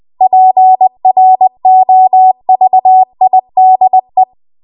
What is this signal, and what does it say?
Morse code, heard as a keyed tone, about 20 wpm, 750 Hz. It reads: PROVIDE
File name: Archive / 20 WPM / 20WPM